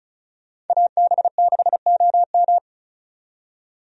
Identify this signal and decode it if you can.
Morse code, heard as a keyed tone, about 34 wpm, 700 Hz. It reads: AB6OM
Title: USA Call Signs - 5 Characters - 35wpm